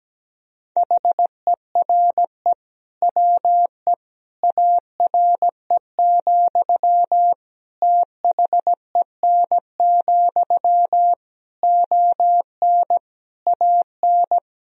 Morse code, heard as a keyed tone, about 17 wpm, 700 Hz. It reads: HERE WE ARE, THEN, ON AN
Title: Sentences from Top 1000 Words - Review Part 3 - 17wpm